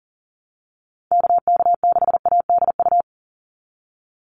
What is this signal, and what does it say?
Morse code, heard as a keyed tone, about 40 wpm, 700 Hz. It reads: KK6ADU
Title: USA Call Signs 40wpm